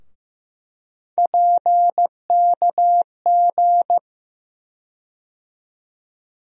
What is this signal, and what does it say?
Morse code, heard as a keyed tone, about 15 wpm, 700 Hz. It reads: PKG
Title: USA Call Sign Postfixes - 3 Letters 15wpm